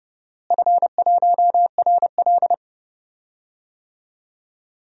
Morse code, heard as a keyed tone, about 31 wpm, 700 Hz. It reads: F1RL
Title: DXCC-100 Call Signs - 4 Characters - Repeated 2x - 30wpm